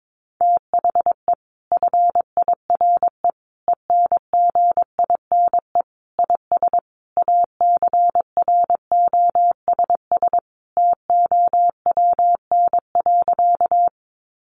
Morse code, heard as a keyed tone, about 22 wpm, 700 Hz. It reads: THE FIRE ENGINE IS ACROSS TOWN.
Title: Sentences from Top 1500 Words 22wpm